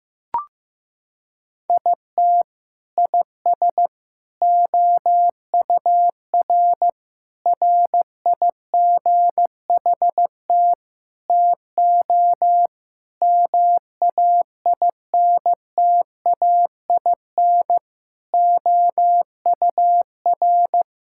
Morse code, heard as a keyed tone, about 15 wpm, 700 Hz. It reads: IT IS OUR RIGHT TO MAINTAIN OUR
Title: Sentences from Top 1000 Words - Review Part 5 - 15wpm